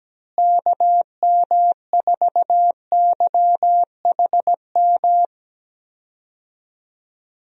Morse code, heard as a keyed tone, about 17 wpm, 700 Hz. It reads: KM4YHM
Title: USA Call Signs - 6 Characters - Repeated 3x - 17wpm